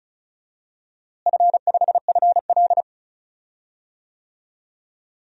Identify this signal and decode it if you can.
Morse code, heard as a keyed tone, about 36 wpm, 700 Hz. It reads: F5FL
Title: DXCC-100 Call Signs - 4 Characters - Repeated 2x - 35wpm